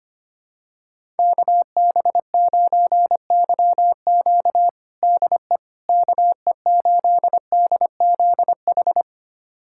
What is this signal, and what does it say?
Morse code, heard as a keyed tone, about 25 wpm, 700 Hz. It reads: KB9YQ DE KE8DZH
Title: USA Callsign DE Callsign 25wpm